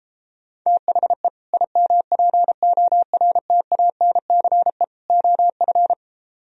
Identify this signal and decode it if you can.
Morse code, heard as a keyed tone, about 33 wpm, 700 Hz. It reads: THE IMPORTANCE OF